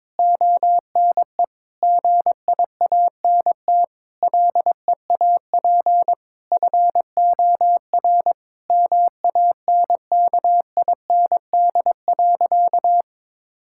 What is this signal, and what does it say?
Morse code, heard as a keyed tone, about 21 wpm, 700 Hz. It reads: ONE GIANT LEAP FOR MANKIND.
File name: Sentences with 9-Word Phrases - 22wpm